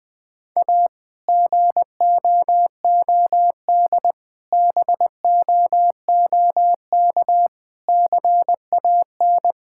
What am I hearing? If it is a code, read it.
Morse code, heard as a keyed tone, about 19 wpm, 700 Hz. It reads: A GOOD BOOK CAN
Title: Difficult Sentences with Hints - 20wpm